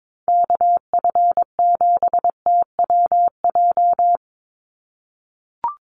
Morse code, heard as a keyed tone, about 22 wpm, 700 Hz. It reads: KF7TWJ
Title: USA Call Signs 22wpm